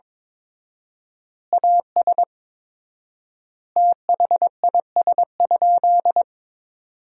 Morse code, heard as a keyed tone, about 21 wpm, 700 Hz. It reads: AS THIS?
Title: 4x Word-Spacing - Sentences from Top 400 Words - 22wpm